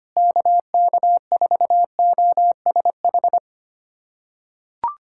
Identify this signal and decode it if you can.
Morse code, heard as a keyed tone, about 25 wpm, 700 Hz. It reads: KK4OSH
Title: USA Callsign DE Callsign 25wpm